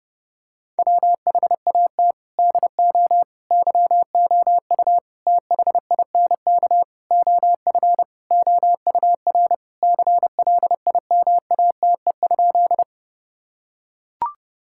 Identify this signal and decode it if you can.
Morse code, heard as a keyed tone, about 32 wpm, 700 Hz. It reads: WHAT DO YOU THINK OF OUR CLIMATE?
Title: Sentences from Top 1600 Words 30wpm